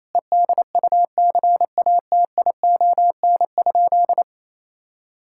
Morse code, heard as a keyed tone, about 27 wpm, 700 Hz. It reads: EDUCATION?
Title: Sentences from Top 1000 Words - Review Part 6 - 28wpm